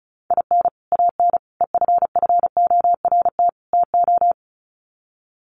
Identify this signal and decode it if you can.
Morse code, heard as a keyed tone, about 34 wpm, 700 Hz. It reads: IN AN EFFORT TO
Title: Sets of 4 Words 35wpm